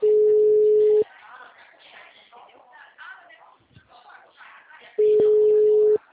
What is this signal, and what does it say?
Ringback tone (Europe, 425 Hz), with speech.